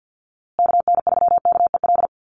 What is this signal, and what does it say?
Morse code, heard as a keyed tone, about 50 wpm, 700 Hz. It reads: KN3KEL
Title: USA Call Signs Repeated 3x 50wpm